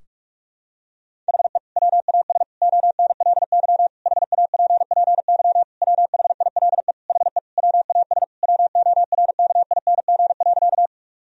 Morse code, heard as a keyed tone, about 46 wpm, 700 Hz. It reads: HE WAS ONLY HAPPY WHILE HE WAS WORKING.
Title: Sentences from Top 1000 Words - Review Part 4 - 45wpm